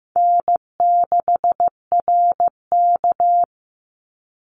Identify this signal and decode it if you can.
Morse code, heard as a keyed tone, about 15 wpm, 700 Hz. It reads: N6RK
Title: Easy Call Signs Repeated 3x - 15wpm